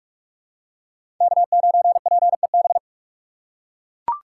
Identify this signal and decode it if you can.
Morse code, heard as a keyed tone, about 43 wpm, 700 Hz. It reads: K9PEB
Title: USA Call Signs - 5 Characters - 4x Character Spacing - 45wpm